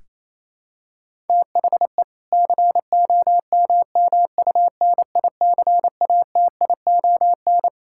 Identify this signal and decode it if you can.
Morse code, heard as a keyed tone, about 27 wpm, 700 Hz. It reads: THE COMMUNICATION
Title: Sentences from Top 1000 Words - Review Part 6 - 28wpm